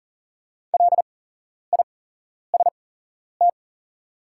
Morse code, heard as a keyed tone, about 38 wpm, spaced as for 11 wpm, 700 Hz. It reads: LIST
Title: 8x Character Spacing - Top 500 Words - 40wpm